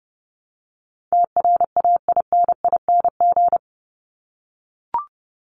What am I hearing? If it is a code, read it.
Morse code, heard as a keyed tone, about 30 wpm, 700 Hz. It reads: TRAINING
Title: Top 501-750 Words 30wpm